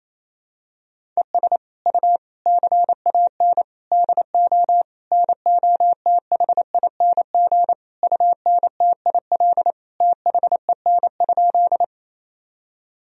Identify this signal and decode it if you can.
Morse code, heard as a keyed tone, about 28 wpm, 700 Hz. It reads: ES U CAN DO NOTHING UNTIL THEN?
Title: Sentences from Top 500 Words and Abbreviations 28wpm